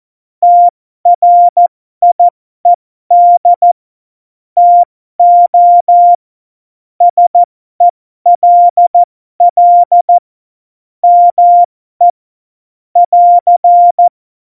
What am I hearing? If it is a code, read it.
Morse code, heard as a keyed tone, about 14 wpm, spaced as for 12 wpm, 700 Hz. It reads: TRIED TO SELL ME <AR>
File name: Morsekurs / Witze